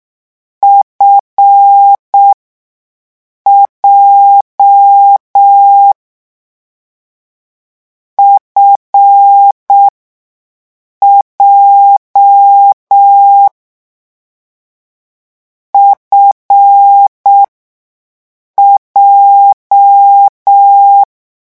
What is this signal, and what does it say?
Morse code, heard as a keyed tone, about 6 wpm, spaced as for 5 wpm, 800 Hz. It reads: FJ FJ FJ
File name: morse mp3 / lecon7-vitesse 6